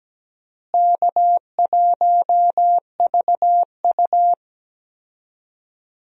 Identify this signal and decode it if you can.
Morse code, heard as a keyed tone, about 17 wpm, 700 Hz. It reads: K1VU
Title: USA Call Signs - 4 Characters - Repeated 3x - 17wpm